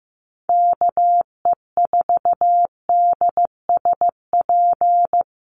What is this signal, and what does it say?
Morse code, heard as a keyed tone, about 15 wpm, 700 Hz. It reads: KE4DSP
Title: USA Call Signs Repeated 3x 15wpm